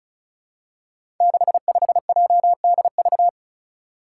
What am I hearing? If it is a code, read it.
Morse code, heard as a keyed tone, about 36 wpm, 700 Hz. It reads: 65JDV
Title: Five Challenging Letter-Number Combinations 35wpm